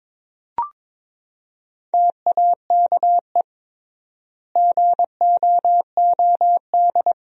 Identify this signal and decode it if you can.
Morse code, heard as a keyed tone, about 22 wpm, 700 Hz. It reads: TAKE GOOD
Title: 3x Word-Spacing - Sentences from Top 500 Words - 22wpm